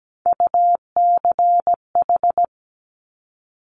Morse code, heard as a keyed tone, about 17 wpm, 700 Hz. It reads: UCH
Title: Top Three Letter Combinations in Words 17wpm